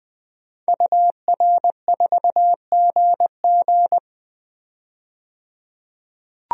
Morse code, heard as a keyed tone, about 20 wpm, 700 Hz. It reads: UR4GG
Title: DXCC-100 Call Signs - 5 Characters - 20wpm